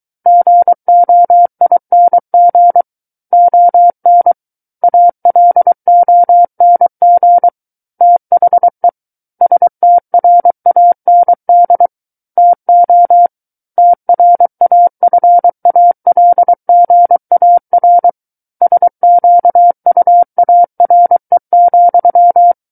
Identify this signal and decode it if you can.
Morse code, heard as a keyed tone, about 23 wpm, 700 Hz. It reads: GOING ON ALONG THE STRAND TO TRAFALGAR SQUARE,
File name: Book One / 14-Chapter 14 - 23 WPM